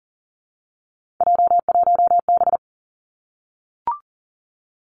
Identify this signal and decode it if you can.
Morse code, heard as a keyed tone, about 40 wpm, 700 Hz. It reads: J1B
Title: Three Challenging Letter-Number Combinations 40wpm